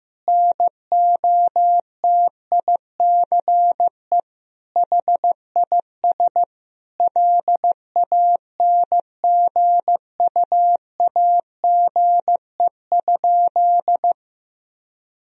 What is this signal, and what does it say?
Morse code, heard as a keyed tone, about 15 wpm, 700 Hz. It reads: NOTICE HIS LANGUAGE?
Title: Sentences from Top 1000 Words - Review Part 3 - 15wpm